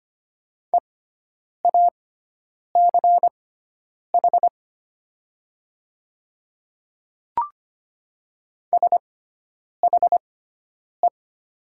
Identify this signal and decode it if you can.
Morse code, heard as a keyed tone, about 25 wpm, spaced as for 9 wpm, 700 Hz. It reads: EACH SHE
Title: 6x Character Spacing - Top 200 Words - 25wpm